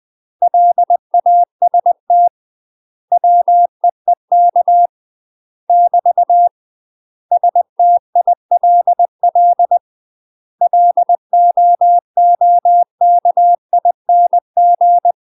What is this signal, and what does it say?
Morse code, heard as a keyed tone, about 20 wpm, 700 Hz. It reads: LAST WEEK <BT> STILL LOOKING
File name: UK MASTER QSOs TRAINING / 20WPM / UK MASTER QSOs 0046